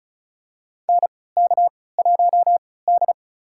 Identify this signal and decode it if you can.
Morse code, heard as a keyed tone, about 36 wpm, spaced as for 20 wpm, 700 Hz. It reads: NK1D
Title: USA Call Signs - 4 Characters - 3x Character Spacing - 35wpm